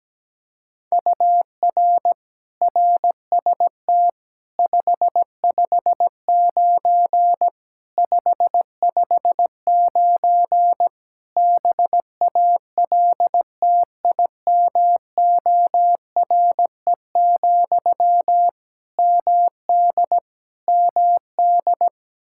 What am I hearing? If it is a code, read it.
Morse code, heard as a keyed tone, about 17 wpm, 700 Hz. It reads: UR RST 559 559 BALTIMORE, MD MD
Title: Practice QSO - Exchange 1 - Beginning - 17wpm